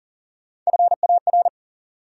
Morse code, heard as a keyed tone, about 40 wpm, 700 Hz. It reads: FAR
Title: Sentences from Top 300 Words - New Word Drill - 40wpm